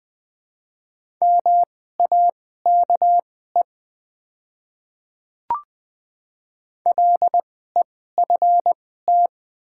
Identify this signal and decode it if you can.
Morse code, heard as a keyed tone, about 20 wpm, spaced as for 15 wpm, 700 Hz. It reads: MAKE LEFT
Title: 2x Character Spacing - Top 300 Words - 20wpm